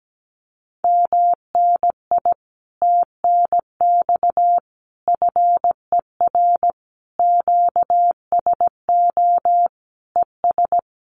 Morse code, heard as a keyed tone, about 17 wpm, 700 Hz. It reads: MNI TNX FER QSO ES
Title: Common QSO Phrases Encore 17wpm